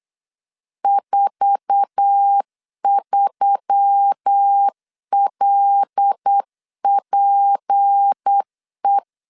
Morse code, heard as a keyed tone, about 8 wpm, 800 Hz. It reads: 43LPE
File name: LEZIONI / 8 YJ45 / 8 YJ4 8wpm 3246